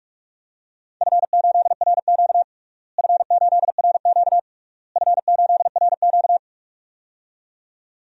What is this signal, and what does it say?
Morse code, heard as a keyed tone, about 47 wpm, 700 Hz. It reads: F8RQ F8RQ F8RQ
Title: DXCC-100 Call Signs - 4 Characters - Repeated 3x - 45wpm